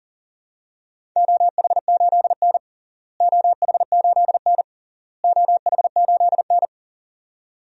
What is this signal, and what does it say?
Morse code, heard as a keyed tone, about 40 wpm, 700 Hz. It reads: OH8N OH8N OH8N
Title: DXCC-100 Call Signs - 4 Characters - Repeated 3x - 40wpm